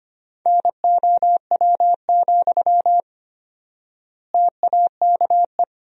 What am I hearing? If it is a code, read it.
Morse code, heard as a keyed tone, about 25 wpm, 700 Hz. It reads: NOW, TAKE
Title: 4x Word-Spacing - Sentences from Top 400 Words - 25wpm